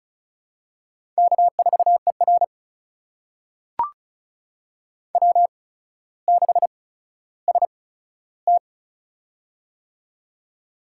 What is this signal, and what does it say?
Morse code, heard as a keyed tone, about 36 wpm, spaced as for 15 wpm, 700 Hz. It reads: K4ER W6ST
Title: USA Call Signs - 4 Characters - 8x Character Spacing - 35wpm